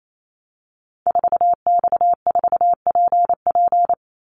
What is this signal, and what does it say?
Morse code, heard as a keyed tone, about 28 wpm, 700 Hz. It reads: 4X4PP
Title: Call Signs - 28wpm